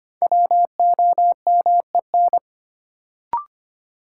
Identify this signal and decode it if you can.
Morse code, heard as a keyed tone, about 25 wpm, 700 Hz. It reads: WOMEN
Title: Sets of 3 Words - New Word Drill - 25wpm